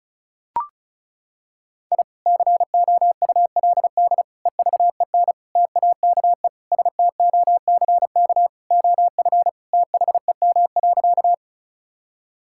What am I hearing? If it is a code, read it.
Morse code, heard as a keyed tone, about 34 wpm, 690 Hz. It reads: I COULD EVEN TAKE STOCK OF THEM.
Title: Sentences from Top 1000 Words - Review Part 5 - 35wpm